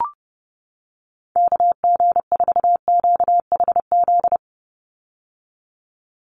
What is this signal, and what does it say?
Morse code, heard as a keyed tone, about 30 wpm, 700 Hz. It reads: KG4QHZ
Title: USA Call Signs 30wpm